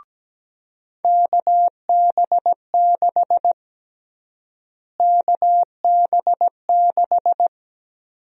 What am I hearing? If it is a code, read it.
Morse code, heard as a keyed tone, about 17 wpm, 700 Hz. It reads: KB6 KB6
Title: USA Call Sign Prefixes - Repeated 2x - 17wpm